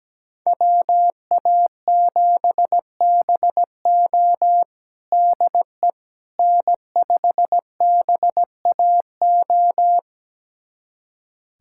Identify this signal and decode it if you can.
Morse code, heard as a keyed tone, about 17 wpm, 700 Hz. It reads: WA7BO DE N5BAO
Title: USA Callsign DE Callsign - 5 Characters - 17wpm